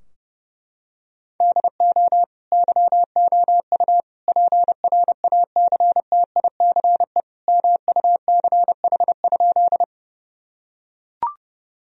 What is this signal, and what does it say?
Morse code, heard as a keyed tone, about 30 wpm, 700 Hz. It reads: DO YOU PRACTICE MUCH?
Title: Sentences from Top 1000 Words - Review Part 4 - 30wpm